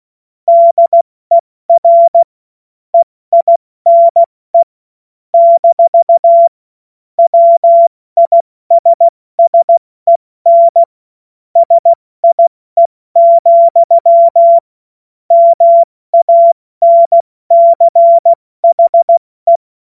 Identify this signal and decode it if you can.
Morse code, heard as a keyed tone, about 16 wpm, 680 Hz. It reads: DER EINE - WISSEN SIE, MANCHE
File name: Morsekurs / Witze / Witz 08. Februar 2022 70 F80